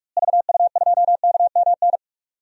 Morse code, heard as a keyed tone, about 46 wpm, 690 Hz. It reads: VU2KMN
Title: DXCC-100 Call Signs - 6 Characters - Repeated 2x - 45wpm